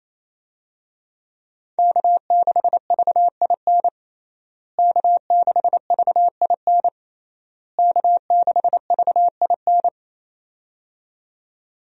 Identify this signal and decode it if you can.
Morse code, heard as a keyed tone, about 28 wpm, 700 Hz. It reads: K6VIN K6VIN K6VIN
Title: USA Call Signs - 5 Characters - Repeated 3x - 28wpm